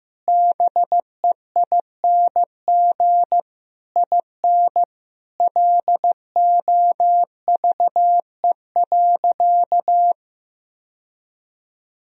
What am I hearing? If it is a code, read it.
Morse code, heard as a keyed tone, about 15 wpm, 700 Hz. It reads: BEING IN LOVE.
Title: Sentences from Top 1000 Words - Review Part 3 - 15wpm